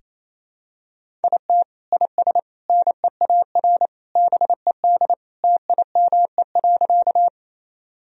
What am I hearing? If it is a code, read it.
Morse code, heard as a keyed tone, about 28 wpm, 700 Hz. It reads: IT IS NEAR BED TIME.
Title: Sentences from Top 1000 Words - Review Part 1 - 28wpm